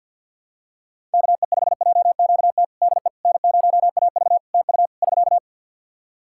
Morse code, heard as a keyed tone, about 54 wpm, 700 Hz. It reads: KE5JQT DE N0AV TU <SK>
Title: Practice QSO - Exchange 4 - Ending - 50wpm